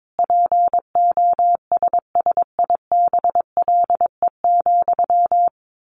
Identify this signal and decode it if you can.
Morse code, heard as a keyed tone, about 22 wpm, 700 Hz. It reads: POSSIBLE,
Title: Sentences from Top 800 Words 22wpm